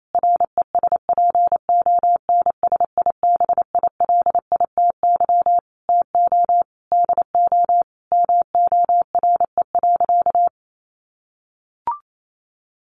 Morse code, heard as a keyed tone, about 28 wpm, 700 Hz. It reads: RESPONSIBILITY TO DO MORE.
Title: Sentences from Top 1400 Words 28wpm